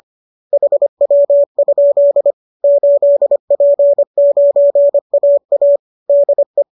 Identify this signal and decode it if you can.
Morse code, heard as a keyed tone, about 25 wpm, 590 Hz. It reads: HW? 8P9AA DE